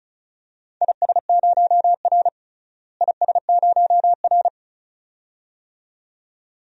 Morse code, heard as a keyed tone, about 36 wpm, 700 Hz. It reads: IS0R IS0R
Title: DXCC-100 Call Signs - 4 Characters - Repeated 2x - 35wpm